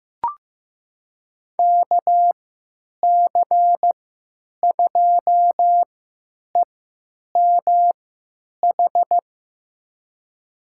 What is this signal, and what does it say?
Morse code, heard as a keyed tone, about 15 wpm, spaced as for 9 wpm, 700 Hz. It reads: KC2EMH
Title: USA Call Signs - 6 Characters - 3x Character Spacing - 15wpm